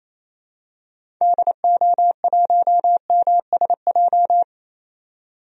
Morse code, heard as a keyed tone, about 28 wpm, 700 Hz. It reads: DO1MSJ
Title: DXCC-100 Call Signs - 6 Characters - 28wpm